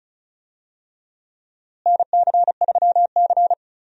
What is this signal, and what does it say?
Morse code, heard as a keyed tone, about 36 wpm, 700 Hz. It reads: NC3C
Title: USA Call Signs - 4 Characters - Repeated 3x - 35wpm